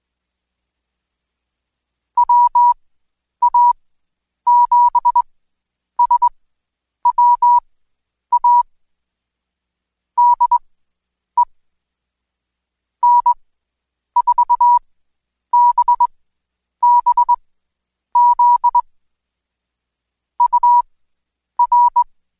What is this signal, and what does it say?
Morse code, heard as a keyed tone, about 21 wpm, spaced as for 9 wpm, 970 Hz. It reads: WA7SWA DE N4BBZ UR
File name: Code Quick / AudioDownld2 / Seg115-Day20-DayDrill 20